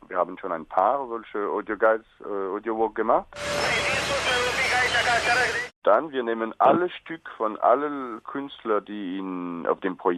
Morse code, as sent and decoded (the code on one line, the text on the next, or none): none